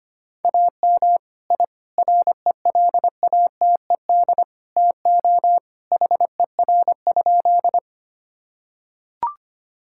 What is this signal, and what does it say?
Morse code, heard as a keyed tone, about 25 wpm, 700 Hz. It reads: AM I RELATED TO HER?
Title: Sentences from Top 1000 Words - Review Part 5 - 25wpm